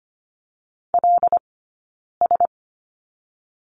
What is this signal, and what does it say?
Morse code, heard as a keyed tone, about 25 wpm, 700 Hz. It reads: L S
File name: Lesson 15d - Number 5 - Two Characters 25wpm